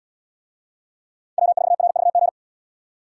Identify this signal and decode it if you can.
Morse code, heard as a keyed tone, about 77 wpm, 700 Hz. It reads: W4RVG